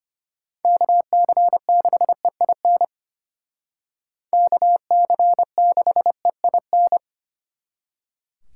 Morse code, heard as a keyed tone, about 30 wpm, 700 Hz. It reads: KC6EIN KC6EIN
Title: Speed Racing - USA 2x3 Call Signs 25wpm